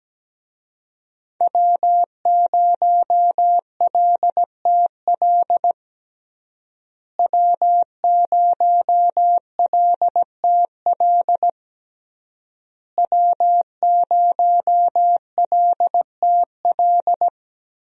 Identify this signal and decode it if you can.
Morse code, heard as a keyed tone, about 17 wpm, 700 Hz. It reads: W0LTL W0LTL W0LTL
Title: USA Call Signs - 5 Characters - Repeated 3x - 17wpm